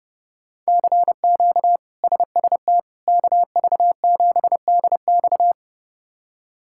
Morse code, heard as a keyed tone, about 29 wpm, 700 Hz. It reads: CQ SST KV7DX
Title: K1USN SST QSO Practice - 30wpm